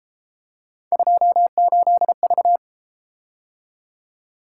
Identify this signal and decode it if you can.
Morse code, heard as a keyed tone, about 33 wpm, 700 Hz. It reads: 28V